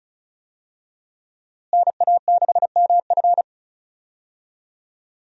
Morse code, heard as a keyed tone, about 34 wpm, 700 Hz. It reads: NA6MF
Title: USA Call Signs - 5 Characters - Repeated 2x - 35wpm